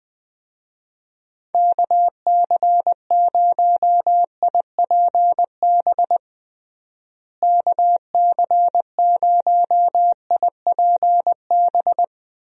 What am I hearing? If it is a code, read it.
Morse code, heard as a keyed tone, about 21 wpm, 700 Hz. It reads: KC0IPB KC0IPB
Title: USA Call Signs - 6 Characters - Repeated 2x - 20wpm